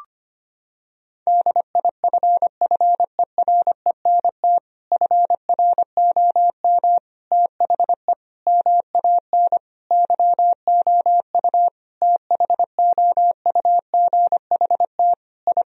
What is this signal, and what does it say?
Morse code, heard as a keyed tone, about 25 wpm, 700 Hz. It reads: DIFFERENT FROM THE MAN YOU THOUGHT I
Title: Sentences from Top 1000 Words - Review Part 3 - 25wpm